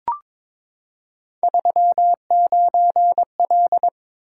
Morse code, heard as a keyed tone, about 22 wpm, 700 Hz. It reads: 39L
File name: practice / 22wpm / Three Letter-Number Combinations - 22wpm